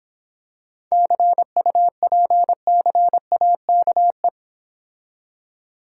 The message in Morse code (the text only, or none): CUPCAKE